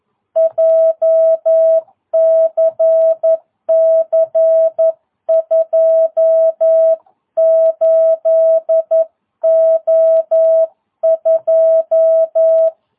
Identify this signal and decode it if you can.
Morse code, heard as a keyed tone, about 11 wpm, 660 Hz. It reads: JCC 28O2
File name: jh8cbh / 2022cw